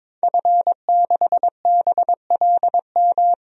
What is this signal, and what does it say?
Morse code, heard as a keyed tone, about 22 wpm, 700 Hz. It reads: F6BLM
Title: DXCC-100 Call Signs - 5 Characters - 22wpm